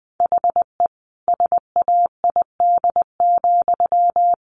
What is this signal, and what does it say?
Morse code, heard as a keyed tone, about 20 wpm, 700 Hz. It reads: HE SAID,
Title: Sentences from Top 600 Words 20wpm